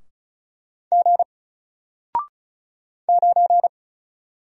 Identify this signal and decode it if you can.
Morse code, heard as a keyed tone, about 35 wpm, 700 Hz. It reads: G 9
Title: Single Letter-Number 35wpm